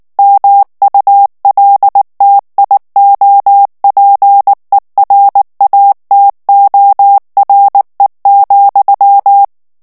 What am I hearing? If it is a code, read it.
Morse code, heard as a keyed tone, about 19 wpm, 800 Hz. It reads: MULTIOPERATORE,
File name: morse mp3 / ITALIEN texte clair 2 vitesse 18 mots minute